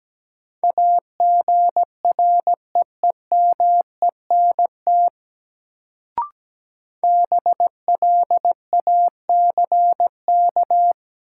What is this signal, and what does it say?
Morse code, heard as a keyed tone, about 17 wpm, 700 Hz. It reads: AGREEMENT BLACK
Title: Sets of 5 Words Encore - New Word Drill - 17wpm